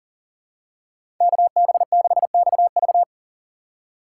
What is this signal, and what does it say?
Morse code, heard as a keyed tone, about 37 wpm, 700 Hz. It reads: KB6XV
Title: USA Call Signs - 5 Characters - 3x Character Spacing - 40wpm